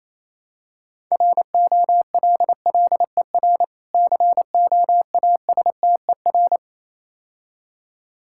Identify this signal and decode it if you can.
Morse code, heard as a keyed tone, about 29 wpm, 700 Hz. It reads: ROLLER COASTER